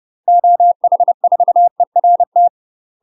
Morse code, heard as a keyed tone, about 30 wpm, 700 Hz. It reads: OH4ERT
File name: UK MASTER QSOs TRAINING / 30WPM / UK MASTER QSOs 0050